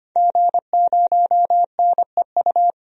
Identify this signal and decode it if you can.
Morse code, heard as a keyed tone, about 25 wpm, 700 Hz. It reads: G0NEU
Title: DXCC-100 Call Signs - 5 Characters - 25wpm